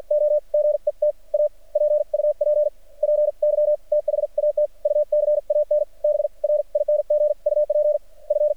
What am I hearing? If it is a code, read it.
Morse code, heard as a keyed tone, about 53 wpm, spaced as for 41 wpm, 600 Hz. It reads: O GET A PUPPY THAT U CAN BRING UP F
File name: archived / CRSnet